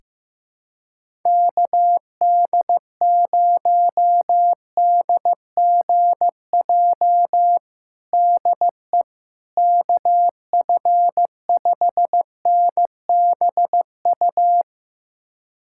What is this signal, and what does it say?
Morse code, heard as a keyed tone, about 15 wpm, 700 Hz. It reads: KD0DGJ DE KF5NBU
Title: USA Callsign DE Callsign - 6 Characters - 15wpm